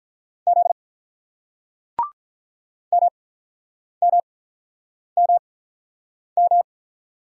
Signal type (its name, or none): Morse code